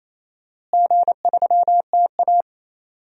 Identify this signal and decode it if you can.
Morse code, heard as a keyed tone, about 28 wpm, 700 Hz. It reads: G3TA
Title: DXCC-100 Call Signs - 4 Characters - Repeated 3x - 28wpm